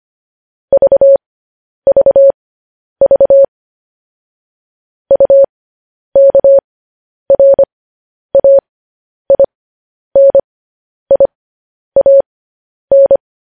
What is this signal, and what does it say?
Morse code, heard as a keyed tone, about 25 wpm, spaced as for 10 wpm, 550 Hz. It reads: VVV UKRAINIAN
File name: podcast / News headlines-1x-25x10